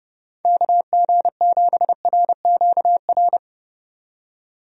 Morse code, heard as a keyed tone, about 30 wpm, 700 Hz. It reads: KG7RQR
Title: USA Call Signs - 6 Characters - 30wpm